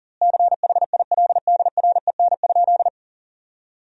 Morse code, heard as a keyed tone, about 40 wpm, 700 Hz. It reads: CHILDREN?
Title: 4x Word-Spacing - Sentences from Top 300 Words - 40wpm